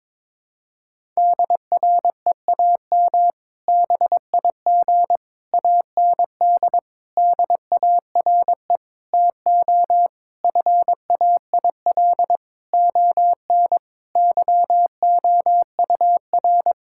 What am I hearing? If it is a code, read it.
Morse code, heard as a keyed tone, about 21 wpm, 700 Hz. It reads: DREAM BIG AND DARE TO FAIL ON YOUR
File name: Difficult Sentences with Hints - 22wpm